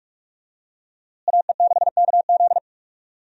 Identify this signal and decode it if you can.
Morse code, heard as a keyed tone, about 47 wpm, 700 Hz. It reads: AE6KZ
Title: USA Call Signs - 5 Characters - Repeated 2x - 45wpm